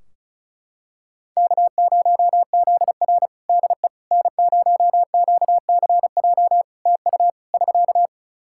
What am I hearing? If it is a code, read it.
Morse code, heard as a keyed tone, about 36 wpm, 700 Hz. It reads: K0ZR DE N0QCJ TU <SK>
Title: Practice QSO - Exchange 4 - Ending - 35wpm